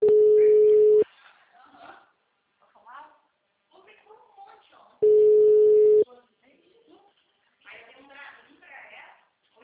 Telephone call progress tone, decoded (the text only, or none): ringback tone